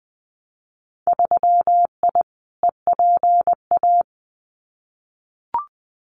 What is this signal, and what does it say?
Morse code, heard as a keyed tone, about 20 wpm, 700 Hz. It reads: 3I EPA
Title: Winter Field Day Exchanges Encore 20wpm